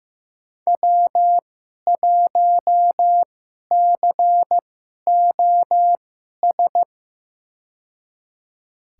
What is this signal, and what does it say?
Morse code, heard as a keyed tone, about 15 wpm, spaced as for 11 wpm, 700 Hz. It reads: W1COS
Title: USA Call Signs - 5 Characters - 2x Character Spacing - 15wpm